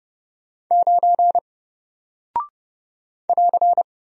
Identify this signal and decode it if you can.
Morse code, heard as a keyed tone, about 30 wpm, 700 Hz. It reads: 9 <AR>
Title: Single Character (Letter:Number:Punctuation:Prosign) Alt 30wpm